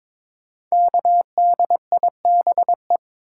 Morse code, heard as a keyed tone, about 22 wpm, 700 Hz. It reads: KDIBE
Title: Code Groups - 5 Letters - 22wpm